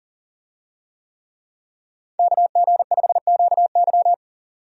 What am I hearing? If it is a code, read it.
Morse code, heard as a keyed tone, about 40 wpm, 700 Hz. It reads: KG5QY
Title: USA Call Signs - 5 Characters - 40wpm